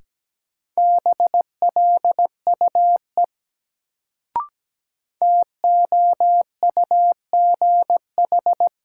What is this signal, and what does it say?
Morse code, heard as a keyed tone, about 17 wpm, 700 Hz. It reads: BLUE TOUGH
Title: Sentences from Top 1200 Words - New Word Drill Alt - 17wpm